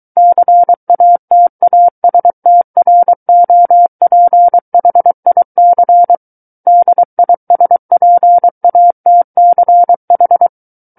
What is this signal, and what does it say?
Morse code, heard as a keyed tone, about 23 wpm, 700 Hz. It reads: CATASTROPHIC DISPATCH
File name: Book One / 14-Chapter 14 - 23 WPM